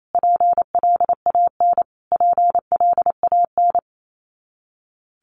Morse code, heard as a keyed tone, about 28 wpm, 700 Hz. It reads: PLAN PLAN
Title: Top 500 Words Repeated 28wpm